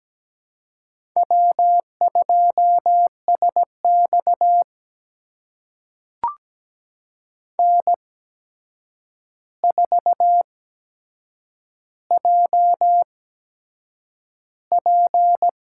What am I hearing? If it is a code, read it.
Morse code, heard as a keyed tone, about 17 wpm, spaced as for 7 wpm, 700 Hz. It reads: W2SX N4JP
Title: USA Call Signs - 4 Characters - 8x Character Spacing - 17wpm